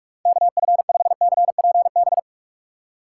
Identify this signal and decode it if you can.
Morse code, heard as a keyed tone, about 44 wpm, 700 Hz. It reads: KF5CPB
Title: USA Call Signs - 6 Characters - 6x Character Spacing - 45wpm